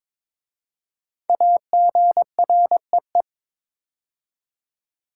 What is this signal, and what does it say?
Morse code, heard as a keyed tone, about 22 wpm, 700 Hz. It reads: AGREE
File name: 5 Letter Words - First 3 Letters Given - 22wpm